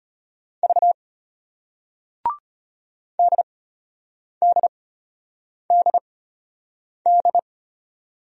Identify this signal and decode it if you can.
Morse code, heard as a keyed tone, about 33 wpm, spaced as for 8 wpm, 700 Hz. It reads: V DDDD